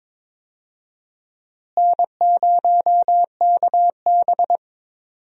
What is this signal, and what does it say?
Morse code, heard as a keyed tone, about 22 wpm, 700 Hz. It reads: N0KB
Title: USA Call Signs - 4 Characters - Repeated 3x - 22wpm